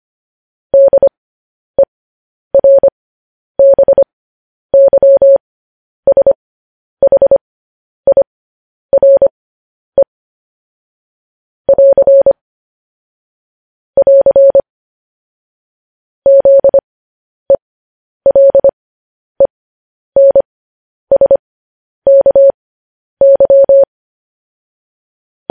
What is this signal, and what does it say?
Morse code, heard as a keyed tone, about 25 wpm, spaced as for 10 wpm, 550 Hz. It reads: DERBYSHIRE <AR> <AR> ZELENSKY
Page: news headlines in morse code at 10 wpm